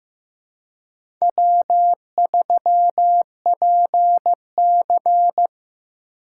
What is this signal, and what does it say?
Morse code, heard as a keyed tone, about 16 wpm, 700 Hz. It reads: W3PC
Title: USA Call Signs - 2x Character Spacing - 15wpm